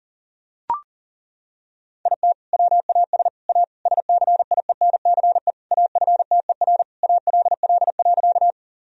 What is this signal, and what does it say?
Morse code, heard as a keyed tone, about 40 wpm, 700 Hz. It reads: IT WAS A SCIENCE AFTER ALL.
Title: Sentences from Top 1000 Words - Review Part 2 - 40wpm